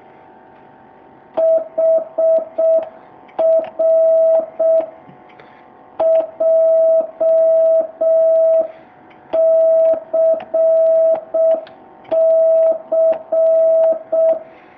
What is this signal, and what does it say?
Morse code, heard as a keyed tone, about 6 wpm, 650 Hz. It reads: HR JCC
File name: jh8cbh / 2019cw